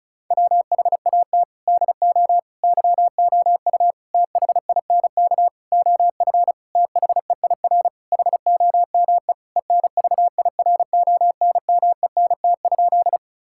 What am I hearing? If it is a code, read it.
Morse code, heard as a keyed tone, about 36 wpm, 700 Hz. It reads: WHAT DO YOU THINK OF THEIR HOME ENVIRONMENT?
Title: Sentences from Top 1000 Words - Review Part 5 - 35wpm